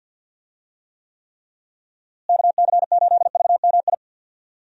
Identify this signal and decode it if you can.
Morse code, heard as a keyed tone, about 52 wpm, 700 Hz. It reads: KC8VMI
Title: USA Call Signs - 6 Characters - Repeated 3x - 50wpm